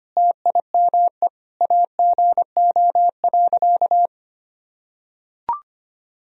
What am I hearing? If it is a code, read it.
Morse code, heard as a keyed tone, about 25 wpm, 700 Hz. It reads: TIME AGO.
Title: Sentences from Top 1000 Words - Review Part 3 - 25wpm